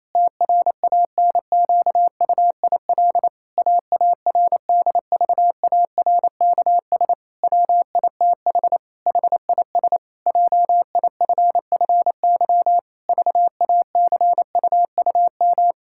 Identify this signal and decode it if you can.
Morse code, heard as a keyed tone, about 28 wpm, 700 Hz. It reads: TRANQUIL AARDVARKS WITH HIS JIFFY VACUUM
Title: Pangram Challenge v2 - 28wpm